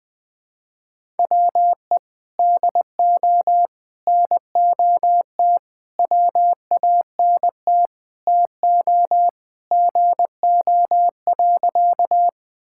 Morse code, heard as a keyed tone, about 21 wpm, 700 Hz. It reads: WE DO NOT WANT TO GO.
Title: Sentences with 4-Word Phrases - 20wpm